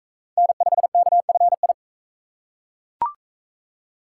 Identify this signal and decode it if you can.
Morse code, heard as a keyed tone, about 37 wpm, 700 Hz. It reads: N5KFI